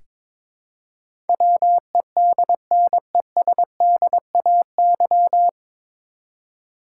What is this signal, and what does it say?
Morse code, heard as a keyed tone, about 22 wpm, 700 Hz. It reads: WEDNESDAY
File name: 9 Letter Words - First 3 Letters Given - 22wpm